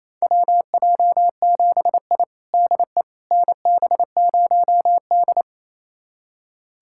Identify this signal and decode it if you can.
Morse code, heard as a keyed tone, about 27 wpm, 700 Hz. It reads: WJ7I DE NB0D
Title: USA Callsign DE Callsign - 4 Characters - 28wpm